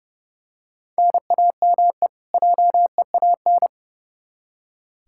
Morse code, heard as a keyed tone, about 30 wpm, 700 Hz. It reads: NAME JEAN
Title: Practice QSO - Exchange 1 - Beginning - 30wpm